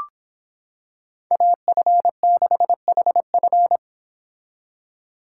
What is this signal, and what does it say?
Morse code, heard as a keyed tone, about 26 wpm, 700 Hz. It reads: AF6HF